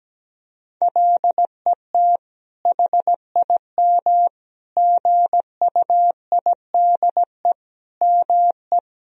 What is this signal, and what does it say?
Morse code, heard as a keyed tone, about 17 wpm, 700 Hz. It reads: LET HIM GUIDE ME
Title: Sentences from Top 1000 Words - Review Part 3 - 17wpm